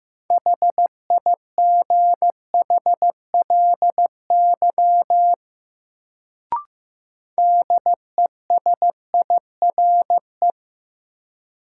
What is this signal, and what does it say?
Morse code, heard as a keyed tone, about 15 wpm, 700 Hz. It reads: HIGHLY DESIRE
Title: Sentences from Top 1100 Words - New Word Drill - 15wpm